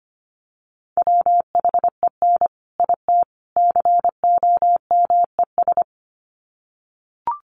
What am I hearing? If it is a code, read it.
Morse code, heard as a keyed tone, about 25 wpm, 700 Hz. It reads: WHEN IT COMES
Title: Sets of 3 Words 25wpm